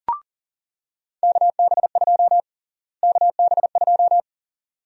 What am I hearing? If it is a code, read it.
Morse code, heard as a keyed tone, about 40 wpm, 700 Hz. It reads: KB2 KB2
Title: USA Call Sign Prefixes - Repeated 2x - 40wpm